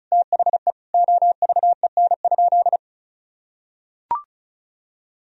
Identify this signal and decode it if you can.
Morse code, heard as a keyed tone, about 36 wpm, 700 Hz. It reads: THE OVEN?
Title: Sentences from Top 1000 Words - Review Part 4 - 35wpm